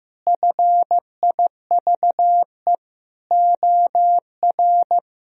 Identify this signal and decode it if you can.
Morse code, heard as a keyed tone, about 15 wpm, 700 Hz. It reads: FIVE OR
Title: Sentences from Top 1000 Words - Review Part 2 - 15wpm